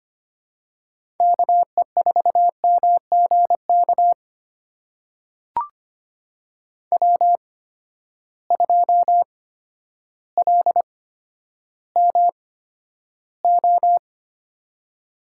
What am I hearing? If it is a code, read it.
Morse code, heard as a keyed tone, about 25 wpm, spaced as for 11 wpm, 700 Hz. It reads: KE4MGK W2LMO
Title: USA Call Signs - 8x Character Spacing - 25wpm